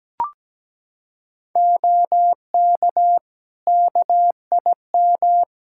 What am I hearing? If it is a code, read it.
Morse code, heard as a keyed tone, about 17 wpm, 700 Hz. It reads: OK KIM
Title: Practice QSO - Exchange 2 - Info - 17wpm